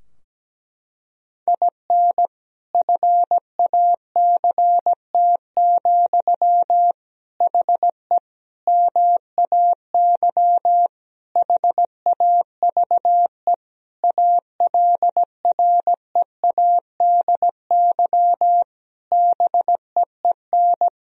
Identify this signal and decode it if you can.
Morse code, heard as a keyed tone, about 17 wpm, 700 Hz. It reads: IN FACT, HE MAY HAVE ALREADY BEEN
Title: Sentences from Top 1000 Words - Review Part 3 - 17wpm